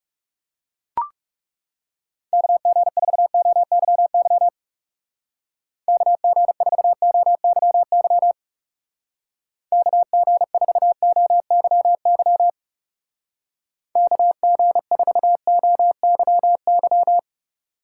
Morse code, heard as a keyed tone, about 35 wpm, 700 Hz. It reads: KG4OYY KG4OYY KG4OYY KG4OYY
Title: Speed Racing - USA 2x3 Call Signs 30wpm